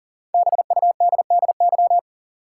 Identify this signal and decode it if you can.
Morse code, heard as a keyed tone, about 40 wpm, 700 Hz. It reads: BUDDY
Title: Five Letter Words - 40wpm